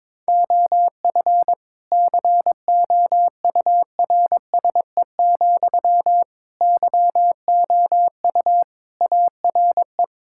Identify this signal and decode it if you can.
Morse code, heard as a keyed tone, about 22 wpm, 700 Hz. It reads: OF COURSE, YOU ARE
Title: Sentences from Top 1000 Words - Review Part 6 - 22wpm